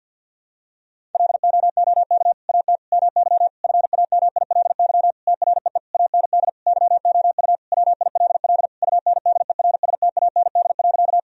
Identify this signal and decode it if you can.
Morse code, heard as a keyed tone, about 51 wpm, 700 Hz. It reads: LOOK AT MY FAMILY TREE AND YOU WILL UNDERSTAND.
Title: Sentences from Top 1000 Words - Review Part 1 - 50wpm